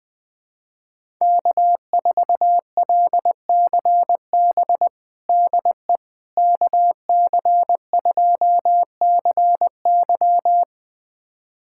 Morse code, heard as a keyed tone, about 20 wpm, 700 Hz. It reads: K4LCB DE KC2CY
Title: USA Callsign DE Callsign - 5 Characters - 20wpm